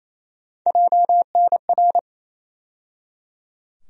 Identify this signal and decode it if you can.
Morse code, heard as a keyed tone, about 28 wpm, 700 Hz. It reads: JNR
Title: USA Call Sign Postfixes - 3 Letters 28wpm